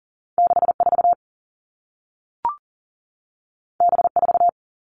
Morse code, heard as a keyed tone, about 39 wpm, 700 Hz. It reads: 64 B4
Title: Two Challenging Letter-Number Combinations 40wpm